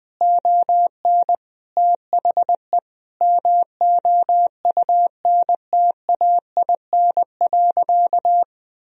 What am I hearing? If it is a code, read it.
Morse code, heard as a keyed tone, about 20 wpm, 700 Hz. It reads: ON THE MOUNTAIN.
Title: Sentences from Top 1000 Words - Review Part 3 - 20wpm